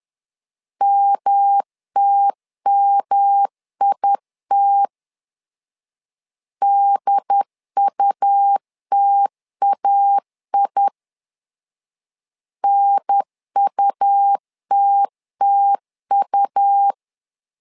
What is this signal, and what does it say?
Morse code, heard as a keyed tone, about 11 wpm, 800 Hz. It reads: MTMIT DUTAI NUTTU
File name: LEZIONI / 3 MWRO / 3 M 10wpm 1828